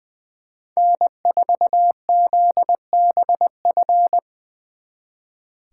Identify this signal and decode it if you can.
Morse code, heard as a keyed tone, about 23 wpm, 700 Hz. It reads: N4ZBF
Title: Speed Racing - USA 1x3 Call Signs 20wpm